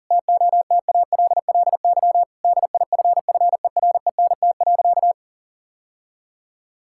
Morse code, heard as a keyed tone, about 40 wpm, 700 Hz. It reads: TOTALLY DIFFERENT.
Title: Sentences from Top 1600 Words 40wpm